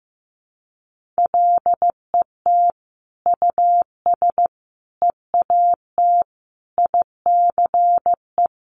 Morse code, heard as a keyed tone, about 14 wpm, 700 Hz. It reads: LET US EAT ICE
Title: Sentences from Top 1500 Words 15wpm